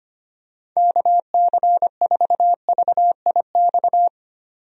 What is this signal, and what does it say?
Morse code, heard as a keyed tone, about 25 wpm, 700 Hz. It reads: KC4VIX
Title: USA Call Signs - 6 Characters - 25wpm